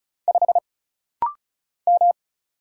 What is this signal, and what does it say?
Morse code, heard as a keyed tone, about 35 wpm, 700 Hz. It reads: 5 M
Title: Single Letter-Number Alt - Rapid-Fire 35wpm